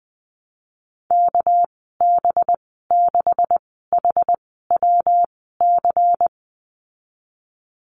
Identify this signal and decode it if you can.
Morse code, heard as a keyed tone, about 20 wpm, spaced as for 15 wpm, 700 Hz. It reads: KB6HWC
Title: 213 Lesson 34g - Number 8 - Call Signs 20wpm